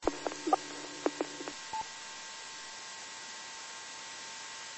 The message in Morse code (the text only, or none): none